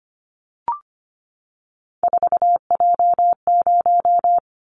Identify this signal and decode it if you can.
Morse code, heard as a keyed tone, about 25 wpm, 700 Hz. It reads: 4J0
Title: Callsign Prefixes for Top 100 DXCC - 25wpm